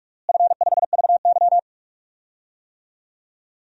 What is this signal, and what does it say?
Morse code, heard as a keyed tone, about 46 wpm, 700 Hz. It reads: F5VY
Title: DXCC-100 Call Signs - 4 Characters - Repeated 3x - 45wpm